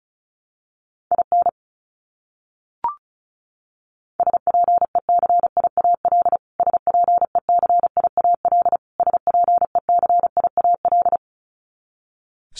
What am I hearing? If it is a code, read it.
Morse code, heard as a keyed tone, about 34 wpm, 700 Hz. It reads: IN SPECIAL SPECIAL SPECIAL
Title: Top 500 Words Repeated 35wpm